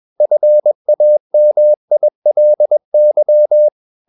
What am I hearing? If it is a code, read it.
Morse code, heard as a keyed tone, about 21 wpm, 600 Hz. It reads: FAMILY
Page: CW med Gnister